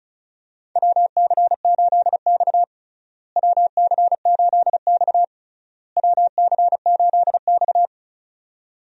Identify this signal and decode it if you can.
Morse code, heard as a keyed tone, about 34 wpm, 700 Hz. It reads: WC8X WC8X WC8X
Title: USA Call Signs - 4 Characters - Repeated 3x - 35wpm